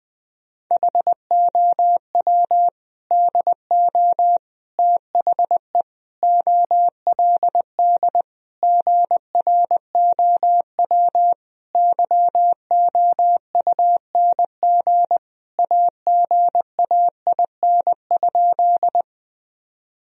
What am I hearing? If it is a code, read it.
Morse code, heard as a keyed tone, about 20 wpm, 700 Hz. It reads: HOW DO THE OLD GROW YOUNG AGAIN?
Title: Sentences from Top 1000 Words - Review Part 1 - 20wpm